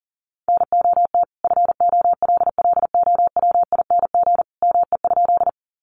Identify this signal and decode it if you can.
Morse code, heard as a keyed tone, about 40 wpm, 700 Hz. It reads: NOT FOLLOWING ME?
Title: Sentences from Top 700 Words 40wpm